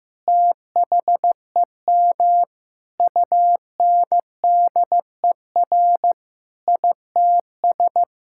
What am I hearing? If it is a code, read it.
Morse code, heard as a keyed tone, about 15 wpm, 700 Hz. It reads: THEM UNDER ITS
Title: Sentences from Top 1600 Words 15wpm